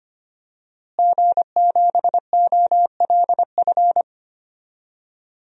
Morse code, heard as a keyed tone, about 25 wpm, 700 Hz. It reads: G7OLF
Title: DXCC-100 Call Signs - 5 Characters - Repeated 2x - 25wpm